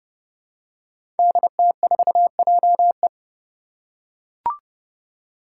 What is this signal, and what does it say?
Morse code, heard as a keyed tone, about 30 wpm, 700 Hz. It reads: DT4JE
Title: Five Challenging Letter-Number Combinations 30wpm